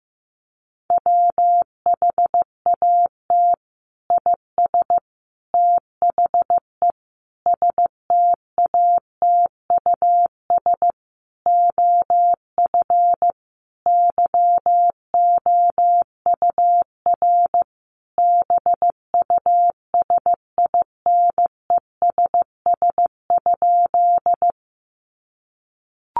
Morse code, heard as a keyed tone, about 15 wpm, 700 Hz. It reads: WHAT IS THE STATUS OF YOUR BUSINESS?
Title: Sentences from Top 1400 Words 15wpm